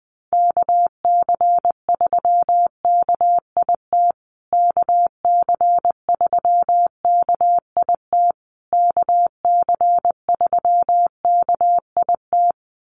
Morse code, heard as a keyed tone, about 20 wpm, 700 Hz. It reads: KC3KIT KC3KIT KC3KIT
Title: USA Call Signs Repeated 3x 20wpm